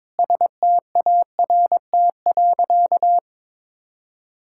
Morse code, heard as a keyed tone, about 22 wpm, 700 Hz. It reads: START.
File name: practice / 22wpm / Sentences with 8-Word Phrases - 22wpm